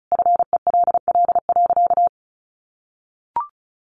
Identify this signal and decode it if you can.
Morse code, heard as a keyed tone, about 34 wpm, 700 Hz. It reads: FELL.
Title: A Sound Of Thunder 35wpm